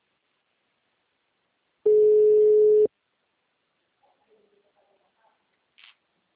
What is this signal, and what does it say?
Ringback tone (Europe, 425 Hz).
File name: out- -2007- - - .29759